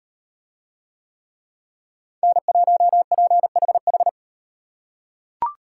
Morse code, heard as a keyed tone, about 38 wpm, 700 Hz. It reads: N1PHH